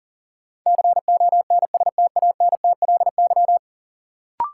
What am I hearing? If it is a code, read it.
Morse code, heard as a keyed tone, about 40 wpm, 700 Hz. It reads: CONSTANTLY
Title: Sentences from Top 1000 Words - Review Part 7 - New Word Drill - 40wpm